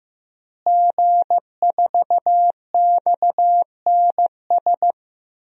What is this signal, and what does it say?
Morse code, heard as a keyed tone, about 15 wpm, 700 Hz. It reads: G4XNS
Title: DXCC-100 Call Signs - 5 Characters - Repeated 2x - 15wpm